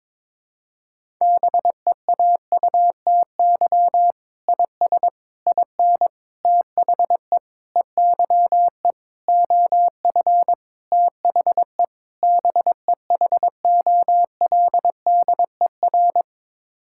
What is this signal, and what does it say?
Morse code, heard as a keyed tone, about 22 wpm, 700 Hz. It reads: BEAUTY IS IN THE EYE OF THE BEHOLDER
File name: practice / 22wpm / Sets of 8 Words - 22wpm